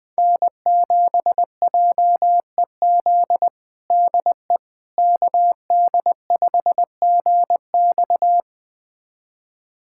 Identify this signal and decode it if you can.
Morse code, heard as a keyed tone, about 20 wpm, 700 Hz. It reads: N7JEZ DE KD5GX
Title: USA Callsign DE Callsign - 5 Characters - 20wpm